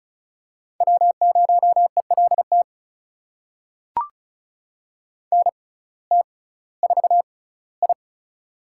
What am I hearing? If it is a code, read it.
Morse code, heard as a keyed tone, about 36 wpm, spaced as for 19 wpm, 700 Hz. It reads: W0ELT NT4I
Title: USA Call Signs - 6x Character Spacing - 35wpm